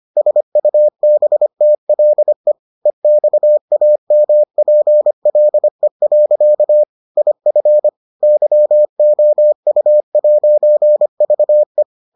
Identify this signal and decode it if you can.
Morse code, heard as a keyed tone, about 25 wpm, 600 Hz. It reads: SUBTLE EXAMPLE. IF YOU'VE